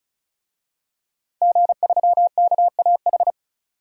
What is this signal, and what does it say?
Morse code, heard as a keyed tone, about 34 wpm, 700 Hz. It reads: G3KAH
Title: DXCC-100 Call Signs - 5 Characters - Repeated 2x - 35wpm